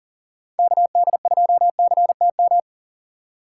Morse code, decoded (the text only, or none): KD2CTM